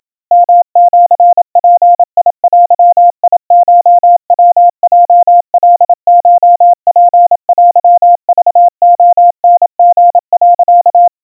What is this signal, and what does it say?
Morse code (Japanese Wabun code), heard as a keyed tone, about 27 wpm, 700 Hz. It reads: ヨシヅデコヤヲカコツテクレタリ、
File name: 100 cpm 700 Hz Crispy